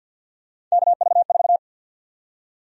Morse code, heard as a keyed tone, about 48 wpm, 700 Hz. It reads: KV4
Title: USA Call Sign Prefixes - Repeated 2x - 50wpm